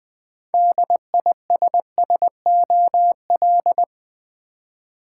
Morse code, heard as a keyed tone, about 20 wpm, 700 Hz. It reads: DISSOL
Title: Build A Word - 3 Letters At A Time - 8-Letter Words - 20wpm